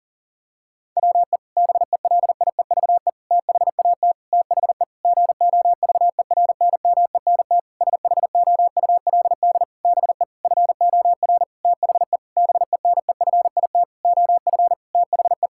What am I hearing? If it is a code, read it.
Morse code, heard as a keyed tone, about 40 wpm, 700 Hz. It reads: WE BELIEVE THAT THE GOVERNMENT SHOULD BE FOR THE BENEFIT OF THE
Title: Sentences from Top 1000 Words - Review Part 6 - 40wpm